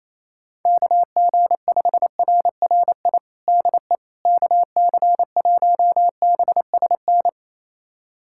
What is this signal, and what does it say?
Morse code, heard as a keyed tone, about 28 wpm, 700 Hz. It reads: KG5RRI DE KC1BSN
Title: USA Callsign DE Callsign 28wpm